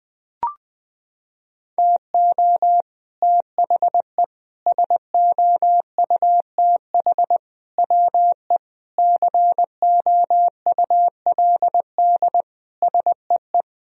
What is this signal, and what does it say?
Morse code, heard as a keyed tone, about 20 wpm, 700 Hz. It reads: TO THE SOUTH WE COULD SEE
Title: Sentences from Top 1000 Words - Review Part 2 - 20wpm